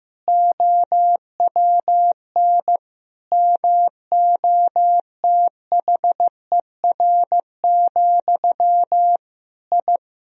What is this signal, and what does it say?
Morse code, heard as a keyed tone, about 15 wpm, 700 Hz. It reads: OWN MOTHER, I
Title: Sentences from Top 1000 Words - Review Part 2 - 15wpm